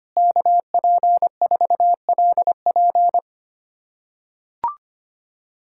Morse code, heard as a keyed tone, about 25 wpm, 700 Hz. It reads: KP4LP
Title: USA Callsign DE Callsign 25wpm